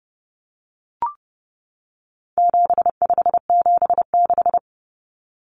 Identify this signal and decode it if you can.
Morse code, heard as a keyed tone, about 30 wpm, 700 Hz. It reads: 7576
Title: Common Cut Numbers - 4-Digit 30wpm